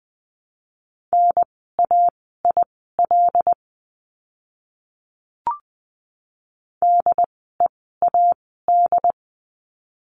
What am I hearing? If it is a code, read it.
Morse code, heard as a keyed tone, about 20 wpm, spaced as for 14 wpm, 700 Hz. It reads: NAIL DEAD
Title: 075 Lesson 13f - Letter L - Words 20wpm